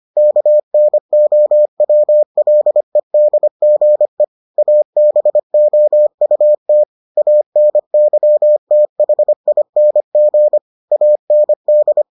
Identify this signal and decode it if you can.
Morse code, heard as a keyed tone, about 25 wpm, 600 Hz. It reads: KNOWLEDGE ABOUT ANYTHING AND